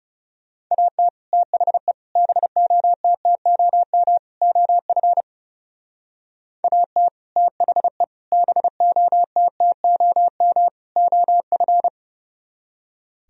Morse code, heard as a keyed tone, about 36 wpm, 700 Hz. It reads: AT THE BOTTOM OF AT THE BOTTOM OF
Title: Speed-Racing - Sets of 4 Words 30wpm